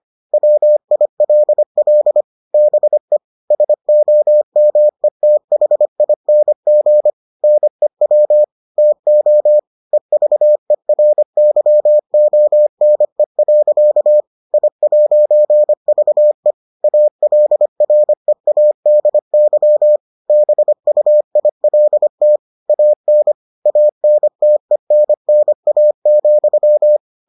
Morse code, heard as a keyed tone, about 25 wpm, 600 Hz. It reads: WILL BE SOMETHING NEW TO EVERYONE. I'VE ALREADY BUILT AN ANTENNA,